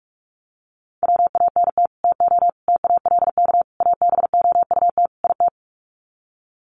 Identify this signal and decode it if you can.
Morse code, heard as a keyed tone, about 44 wpm, 700 Hz. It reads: WANT TO TALK ABOUT IT
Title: Sets of 5 Words Encore 45wpm